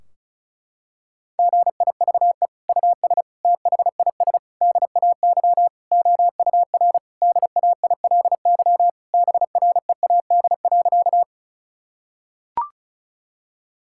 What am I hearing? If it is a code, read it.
Morse code, heard as a keyed tone, about 36 wpm, 700 Hz. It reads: GIVE US THIS DAY OUR DAILY BREAD.
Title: Sentences from Top 1600 Words 35wpm